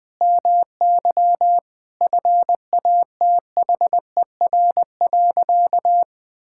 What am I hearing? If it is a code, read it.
Morse code, heard as a keyed tone, about 20 wpm, 700 Hz. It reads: MY FATHER.
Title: Sentences from Top 1000 Words - Review Part 5 - 20wpm